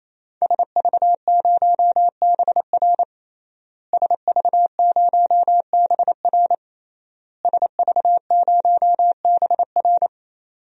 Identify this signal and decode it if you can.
Morse code, heard as a keyed tone, about 28 wpm, 690 Hz. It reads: SV0BR SV0BR SV0BR
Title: DXCC-100 Call Signs - 5 Characters - Repeated 3x - 28wpm